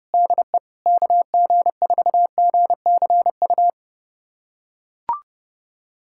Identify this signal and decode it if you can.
Morse code, heard as a keyed tone, about 32 wpm, 700 Hz. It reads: DE KG4GCU
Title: USA Callsign DE Callsign 30wpm